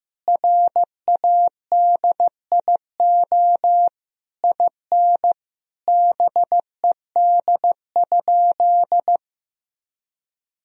Morse code, heard as a keyed tone, about 15 wpm, 700 Hz. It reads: RADIO IN BED?
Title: Sentences from Top 1000 Words - Review Part 3 - 15wpm